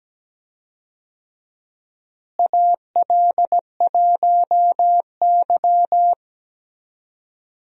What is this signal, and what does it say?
Morse code, heard as a keyed tone, about 17 wpm, 700 Hz. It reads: AL1Y
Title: USA Call Signs - 4 Characters - Repeated 3x - 17wpm